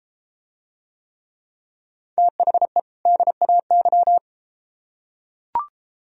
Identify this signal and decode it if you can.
Morse code, heard as a keyed tone, about 32 wpm, 700 Hz. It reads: THE DAY